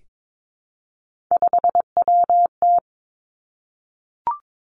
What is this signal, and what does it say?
Morse code, heard as a keyed tone, about 22 wpm, 700 Hz. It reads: 5WT
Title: Three Challenging Letter-Number Combinations 22wpm